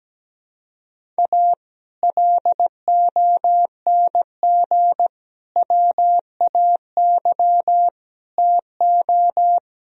Morse code, heard as a keyed tone, about 17 wpm, 700 Hz. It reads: A LONG WAY TO
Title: Speed-Racing - Sets of 5 Words 17wpm